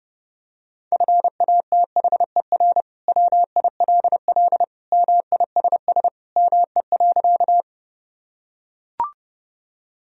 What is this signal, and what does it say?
Morse code, heard as a keyed tone, about 30 wpm, 700 Hz. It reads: FATHER WILL MISS ME.
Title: Sentences from Top 1000 Words - Review Part 3 - 30wpm